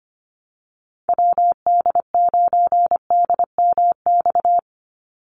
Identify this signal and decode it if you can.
Morse code, heard as a keyed tone, about 25 wpm, 700 Hz. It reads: WD9DMX
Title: USA Call Signs 25wpm